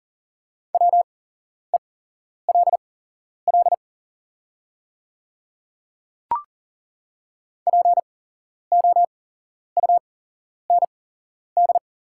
Morse code, heard as a keyed tone, about 40 wpm, spaced as for 11 wpm, 700 Hz. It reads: WELL POUND
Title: 8x Character Spacing - Top 500 Words - 40wpm